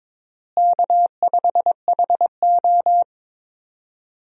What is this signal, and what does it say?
Morse code, heard as a keyed tone, about 22 wpm, 700 Hz. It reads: K5HO
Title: USA Call Signs - 4 Characters - Repeated 3x - 22wpm